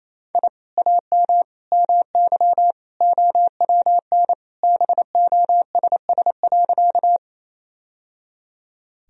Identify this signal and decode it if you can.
Morse code, heard as a keyed tone, about 28 wpm, 700 Hz. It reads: I AM MY OWN BOSS.
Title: Sentences from Top 1000 Words - Review Part 4 - 28wpm